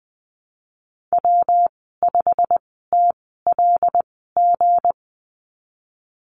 Morse code, heard as a keyed tone, about 20 wpm, spaced as for 15 wpm, 700 Hz. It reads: W5TLG
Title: 167 Lesson 27g - Number 7 - Call Signs 20wpm